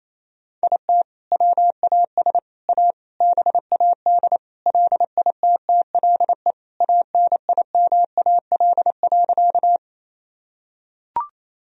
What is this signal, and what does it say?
Morse code, heard as a keyed tone, about 28 wpm, 700 Hz. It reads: IT WAS A BAD LITTLE ANIMAL.
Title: Sentences from Top 1000 Words - Review Part 1 - 28wpm